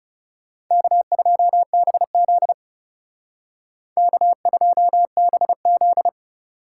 Morse code, heard as a keyed tone, about 34 wpm, 700 Hz. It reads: K2BZ K2BZ
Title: Speed Racing - USA Call Signs 30wpm